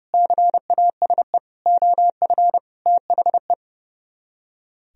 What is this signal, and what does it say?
Morse code, heard as a keyed tone, about 38 wpm, 700 Hz. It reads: CASE OF THE